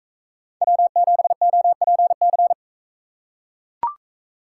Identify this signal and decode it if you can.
Morse code, heard as a keyed tone, about 39 wpm, 700 Hz. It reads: W7OPC